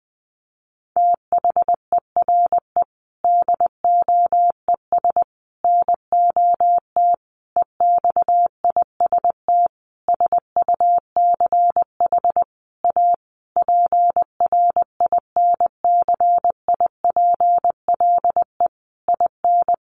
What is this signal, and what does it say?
Morse code, heard as a keyed tone, about 20 wpm, 700 Hz. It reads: THERE DOES NOT EXIST SUCH A PRINCIPLE IN
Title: Sentences from Top 1400 Words 20wpm